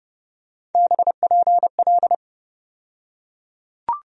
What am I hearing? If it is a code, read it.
Morse code, heard as a keyed tone, about 30 wpm, 700 Hz. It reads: BPL
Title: USA Call Sign Postfixes - 3 Letters 30wpm